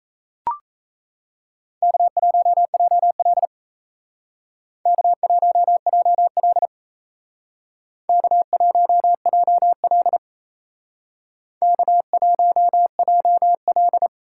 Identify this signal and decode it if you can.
Morse code, heard as a keyed tone, about 34 wpm, 700 Hz. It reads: K1JL K1JL K1JL K1JL
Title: Speed Racing - USA 1x2 Call Signs 28wpm